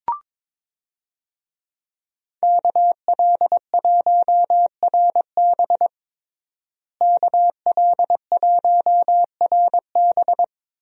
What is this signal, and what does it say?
Morse code, heard as a keyed tone, about 22 wpm, 700 Hz. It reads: KL1RB KL1RB
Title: USA Call Signs - 5 Characters - Repeated 3x - 22wpm